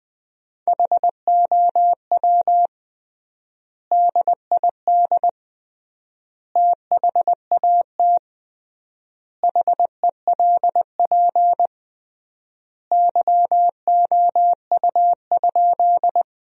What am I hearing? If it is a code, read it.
Morse code, heard as a keyed tone, about 20 wpm, 700 Hz. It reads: HOW DID THAT HELP YOU?
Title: 3x Word-Spacing - Sentences from Top 200 Words - 20wpm